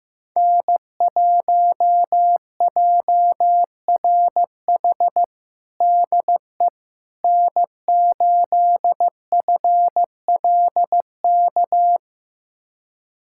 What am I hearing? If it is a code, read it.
Morse code, heard as a keyed tone, about 15 wpm, 700 Hz. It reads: N1JRH DE N8FLK
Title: USA Callsign DE Callsign - 5 Characters - 15wpm